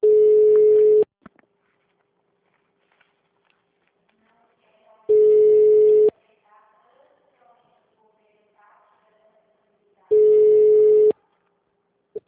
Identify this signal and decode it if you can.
Ringback tone (Europe, 425 Hz), with speech.
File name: out- -2016- - - .6